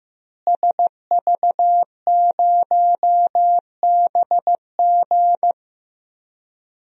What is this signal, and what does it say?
Morse code, heard as a keyed tone, about 15 wpm, 700 Hz. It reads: SV0BG
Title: DXCC-100 Call Signs - 5 Characters - Repeated 3x - 15wpm